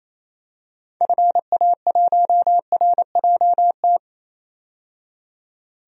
Morse code, heard as a keyed tone, about 28 wpm, 700 Hz. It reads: FA1RJT
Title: DXCC-100 Call Signs - 6 Characters - 28wpm